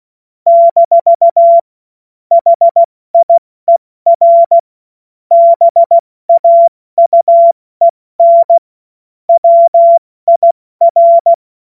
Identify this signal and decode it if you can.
Morse code, heard as a keyed tone, about 16 wpm, 680 Hz. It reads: - HIER BAUEN WIR
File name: Morsekurs / Witze / Witz 23. Dezember 2025 70 F80